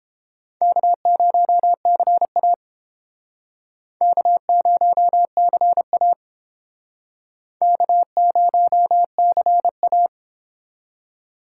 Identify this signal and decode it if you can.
Morse code, heard as a keyed tone, about 32 wpm, 700 Hz. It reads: K0CA K0CA K0CA